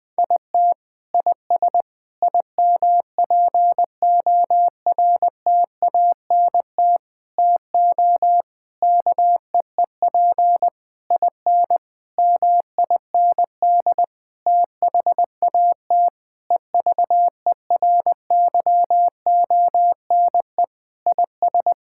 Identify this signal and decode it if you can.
Morse code, heard as a keyed tone, about 20 wpm, 700 Hz. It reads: IT IS IMPORTANT TO KEEP IN MIND THAT EVERYONE IS
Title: Sentences with 8-Word Phrases - 20wpm